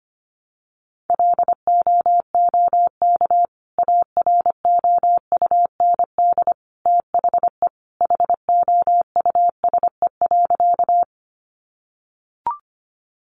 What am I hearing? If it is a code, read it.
Morse code, heard as a keyed tone, about 25 wpm, 700 Hz. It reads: LOOK AROUND THE HOUSE.
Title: Sentences from Top 600 Words 25wpm